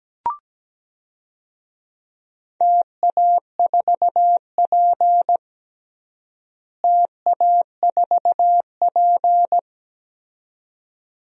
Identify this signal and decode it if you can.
Morse code, heard as a keyed tone, about 17 wpm, 700 Hz. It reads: TA4P TA4P
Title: DXCC-100 Call Signs - 4 Characters - Repeated 2x - 17wpm